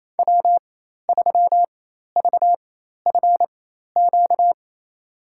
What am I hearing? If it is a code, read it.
Morse code, heard as a keyed tone, about 28 wpm, spaced as for 13 wpm, 700 Hz. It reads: W3VFQ
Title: USA Call Signs - 5 Characters - 4x Character Spacing - 28wpm